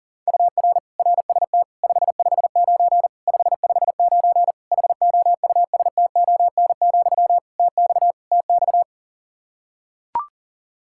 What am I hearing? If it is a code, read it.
Morse code, heard as a keyed tone, about 40 wpm, 700 Hz. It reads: UR RST 559 559 HOUSTON, TX TX
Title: K1USN SST QSO Practice 40wpm